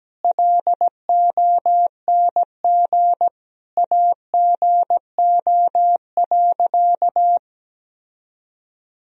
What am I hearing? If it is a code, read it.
Morse code, heard as a keyed tone, about 17 wpm, 700 Hz. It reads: LONG AGO.
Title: Sentences from Top 1000 Words - Review Part 3 - 17wpm